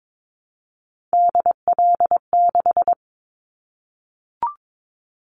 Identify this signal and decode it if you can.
Morse code, heard as a keyed tone, about 22 wpm, 700 Hz. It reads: DL6
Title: Callsign Prefixes for Top 25 DXCC - 22wpm